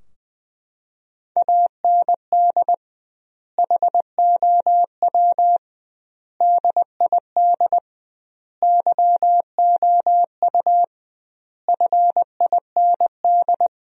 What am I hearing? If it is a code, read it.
Morse code, heard as a keyed tone, about 21 wpm, 700 Hz. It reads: AND HOW DID YOU FIND
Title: 2x Word-Spacing - Sentences from Top 100 Words - 20wpm